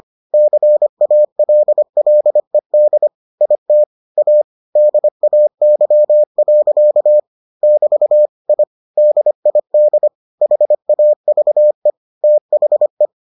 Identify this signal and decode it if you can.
Morse code, heard as a keyed tone, about 25 wpm, 600 Hz. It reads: CALLED IT A DAY. <BT> I DID HAVE THE